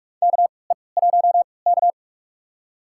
Morse code, heard as a keyed tone, about 47 wpm, spaced as for 26 wpm, 700 Hz. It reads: KE1K
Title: USA Call Signs - 3x Character Spacing - 45wpm